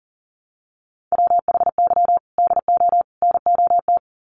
Morse code, heard as a keyed tone, about 40 wpm, 700 Hz. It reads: WHY DO NOT
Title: Sets of 3 Words 40wpm